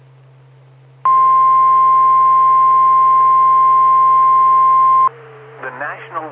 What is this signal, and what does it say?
Emergency Alert System (SAME) tones (the 1050 Hz attention tone), with speech. It is heard from 1.0 to 5.1 s.